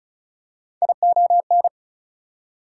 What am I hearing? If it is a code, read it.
Morse code, heard as a keyed tone, about 34 wpm, 700 Hz. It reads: ION
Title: Prefixes and Suffixes 35wpm